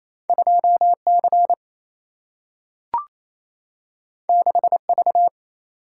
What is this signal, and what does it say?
Morse code, heard as a keyed tone, about 28 wpm, 700 Hz. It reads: 2 C 6 V
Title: Single-Letter Single-Number 28wpm